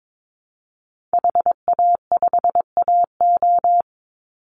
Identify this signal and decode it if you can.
Morse code, heard as a keyed tone, about 22 wpm, 700 Hz. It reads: HA5AO
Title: Call Signs - 22wpm